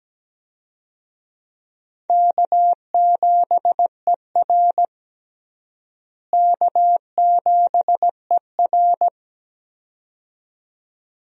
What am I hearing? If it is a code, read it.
Morse code, heard as a keyed tone, about 17 wpm, 700 Hz. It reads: K7ER K7ER
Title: USA Call Signs - 4 Characters - Repeated 2x - 17wpm